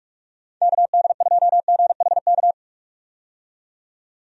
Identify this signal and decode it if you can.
Morse code, heard as a keyed tone, about 42 wpm, 700 Hz. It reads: KD2GHK
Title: USA Call Signs - 6 Characters - 45wpm